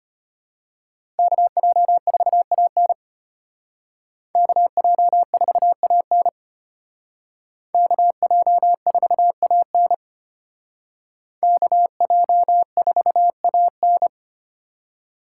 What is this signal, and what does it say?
Morse code, heard as a keyed tone, about 32 wpm, 700 Hz. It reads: KJ4AN KJ4AN KJ4AN KJ4AN